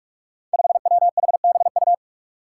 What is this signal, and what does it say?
Morse code, heard as a keyed tone, about 46 wpm, 700 Hz. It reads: 5WHBU
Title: Five Challenging Letter-Number Combinations 45wpm